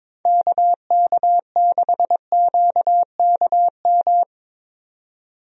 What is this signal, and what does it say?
Morse code, heard as a keyed tone, about 22 wpm, 700 Hz. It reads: KK6QKM
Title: USA Call Signs - 6 Characters - 22wpm